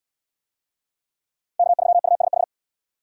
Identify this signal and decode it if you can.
Morse code, heard as a keyed tone, about 68 wpm, 700 Hz. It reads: K3USH